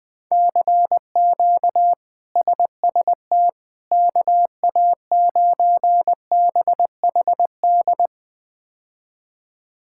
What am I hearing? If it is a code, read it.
Morse code, heard as a keyed tone, about 20 wpm, 700 Hz. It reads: CQ SST KA9BHD
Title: K1USN SST QSO Practice - 20wpm